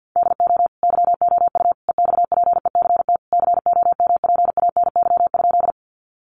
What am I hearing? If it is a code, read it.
Morse code, heard as a keyed tone, about 48 wpm, 700 Hz. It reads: DO YOU EXPECT COMPANY?
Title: Sentences from Top 1100 Words 50wpm